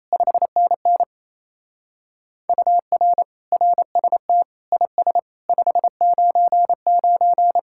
Morse code, heard as a keyed tone, about 38 wpm, 700 Hz. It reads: 5NN UR RST IS 599